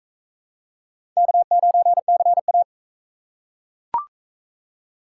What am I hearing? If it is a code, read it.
Morse code, heard as a keyed tone, about 41 wpm, 700 Hz. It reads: K9CA